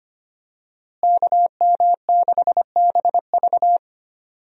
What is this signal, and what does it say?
Morse code, heard as a keyed tone, about 25 wpm, 700 Hz. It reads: KM6BV
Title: USA Call Signs - 5 Characters - Repeated 2x - 25wpm